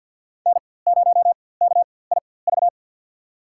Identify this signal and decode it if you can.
Morse code, heard as a keyed tone, about 47 wpm, spaced as for 23 wpm, 700 Hz. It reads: N0KIV
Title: USA Call Signs - 5 Characters - 4x Character Spacing - 50wpm